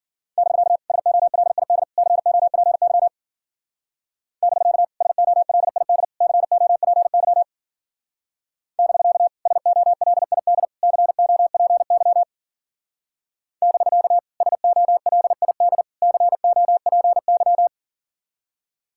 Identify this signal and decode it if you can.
Morse code, heard as a keyed tone, about 49 wpm, 700 Hz. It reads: <BK> SOLID COPY <BK> SOLID COPY <BK> SOLID COPY <BK> SOLID COPY
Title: Speed-Racing Common QSO Phrases Encore - 40wpm